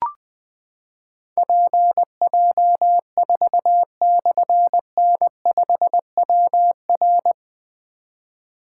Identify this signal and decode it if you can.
Morse code, heard as a keyed tone, about 20 wpm, 700 Hz. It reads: PJ4/N5WR
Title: Difficult Call Signs 20wpm